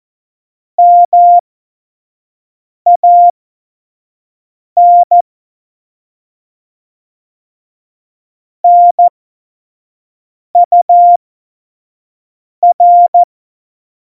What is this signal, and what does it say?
Morse code, heard as a keyed tone, about 14 wpm, spaced as for 5 wpm, 700 Hz. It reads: MAN NUR
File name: Morsekurs / Witze / Witz 19 Mai 2020, 25 F70